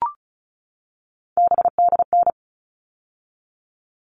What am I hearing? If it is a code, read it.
Morse code, heard as a keyed tone, about 34 wpm, 700 Hz. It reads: BDN
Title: All Cut Numbers - 3-Digit 35wpm